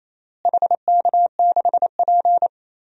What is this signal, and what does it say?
Morse code, heard as a keyed tone, about 28 wpm, 700 Hz. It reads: HK6P
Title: DXCC-100 Call Signs - 4 Characters - Repeated 3x - 28wpm